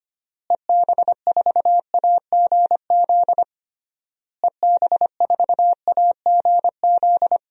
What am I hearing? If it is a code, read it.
Morse code, heard as a keyed tone, about 25 wpm, 700 Hz. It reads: EB4AGZ EB4AGZ
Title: DXCC-100 Call Signs - 6 Characters - Repeated 3x - 25wpm